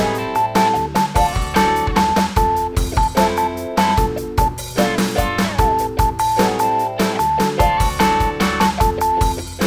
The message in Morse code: LOST E YOU